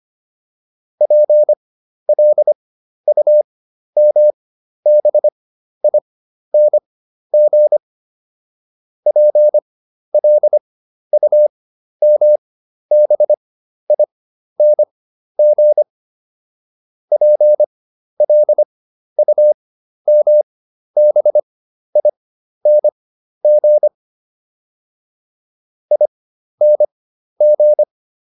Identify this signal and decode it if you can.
Morse code, heard as a keyed tone, about 25 wpm, spaced as for 12 wpm, 600 Hz. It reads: PLUMBING PLUMBING PLUMBING ING